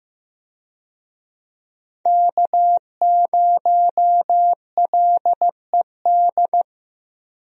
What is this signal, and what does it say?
Morse code, heard as a keyed tone, about 15 wpm, 700 Hz. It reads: K0LED
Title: USA Call Signs - 5 Characters - Repeated 2x - 15wpm